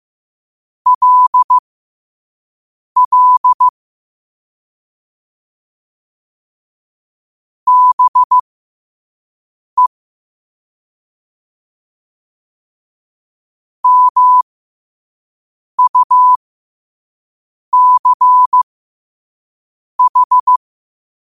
Morse code, heard as a keyed tone, about 15 wpm, spaced as for 5 wpm, 1000 Hz. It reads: LL BE MUCH
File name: qotdfiles / 5wpm